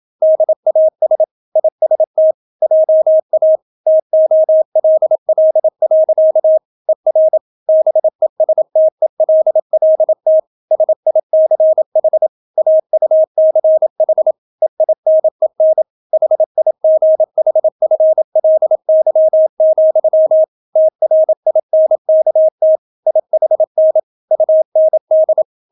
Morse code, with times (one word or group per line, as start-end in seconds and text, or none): 0.2-1.2 DAS
1.6-2.3 IST
2.6-3.6 JA
3.9-6.6 TOLL.
6.9-7.4 ER
7.7-10.4 BESTELLT
10.7-12.3 SICH
12.6-14.3 AUCH
14.6-15.8 EINEN
16.1-20.4 HIGHFLY,
20.8-22.8 TRINKT
23.1-24.0 IHN
24.3-25.4 UND